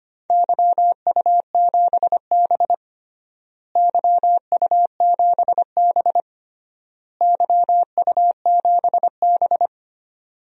Morse code, heard as a keyed tone, about 25 wpm, 700 Hz. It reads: YU7B YU7B YU7B
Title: DXCC-100 Call Signs - 4 Characters - Repeated 3x - 25wpm